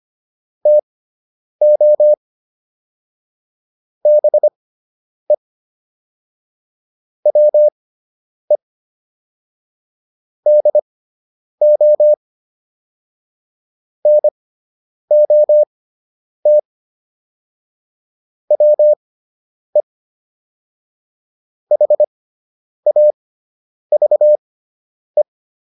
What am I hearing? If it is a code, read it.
Morse code, heard as a keyed tone, about 25 wpm, spaced as for 9 wpm, 600 Hz. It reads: TO BE WE DO NOT WE HAVE